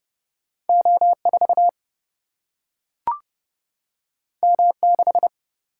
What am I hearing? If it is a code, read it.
Morse code, heard as a keyed tone, about 30 wpm, 700 Hz. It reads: O4 M6
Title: Single-Letter Single-Number 30wpm